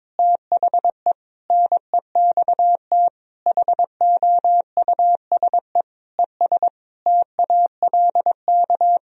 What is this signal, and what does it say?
Morse code, heard as a keyed tone, about 22 wpm, 700 Hz. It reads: THE NEXT HOUSE ES TALK
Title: Sentences from Top 400 Words and Abbreviations 22wpm